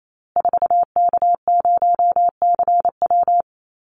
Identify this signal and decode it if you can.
Morse code, heard as a keyed tone, about 28 wpm, 700 Hz. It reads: 4K0CW
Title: Call Signs Repeated 3x - 28wpm